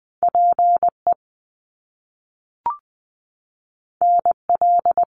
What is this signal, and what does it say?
Morse code, heard as a keyed tone, about 20 wpm, 700 Hz. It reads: PE NL
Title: Canadian Provinces and Territories 20wpm